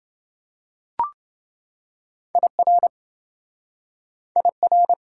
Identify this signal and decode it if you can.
Morse code, heard as a keyed tone, about 28 wpm, 700 Hz. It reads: IR IR